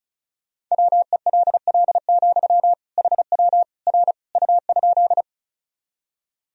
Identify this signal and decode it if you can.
Morse code, heard as a keyed tone, about 36 wpm, 700 Hz. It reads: WELL, HW R U?
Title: Sentences from Top 200 Words and Abbreviations 35wpm